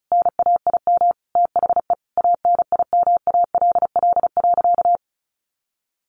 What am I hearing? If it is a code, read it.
Morse code, heard as a keyed tone, about 34 wpm, 700 Hz. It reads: NAIM THE ANIMALL.
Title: A Sound Of Thunder 35wpm